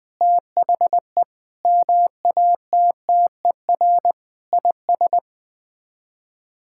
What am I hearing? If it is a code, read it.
Morse code, heard as a keyed tone, about 21 wpm, 700 Hz. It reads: THE MATTER IS
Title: Sets of 6 Words - 20wpm